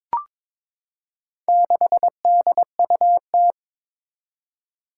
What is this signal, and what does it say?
Morse code, heard as a keyed tone, about 23 wpm, 700 Hz. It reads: 6DUT
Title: Four Challenging Letter-Number Combinations 22wpm